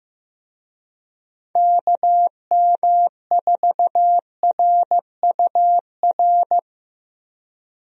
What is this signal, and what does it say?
Morse code, heard as a keyed tone, about 15 wpm, 700 Hz. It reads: KM4RUR
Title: USA Call Signs - 6 Characters - 15wpm